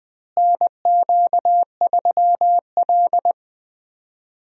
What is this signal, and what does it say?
Morse code, heard as a keyed tone, about 21 wpm, 700 Hz. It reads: NQ3L
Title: USA Call Signs - 4 Characters - Repeated 2x - 20wpm